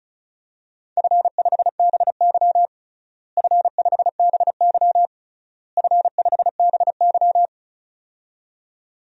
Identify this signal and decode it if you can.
Morse code, heard as a keyed tone, about 36 wpm, 700 Hz. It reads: F5BY F5BY F5BY
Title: DXCC-100 Call Signs - 4 Characters - Repeated 3x - 35wpm